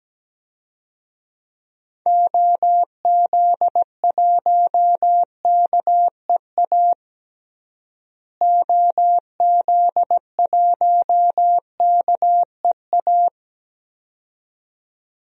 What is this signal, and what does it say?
Morse code, heard as a keyed tone, about 17 wpm, 700 Hz. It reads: OZ1KEA OZ1KEA
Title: DXCC-100 Call Signs - 6 Characters - Repeated 2x - 17wpm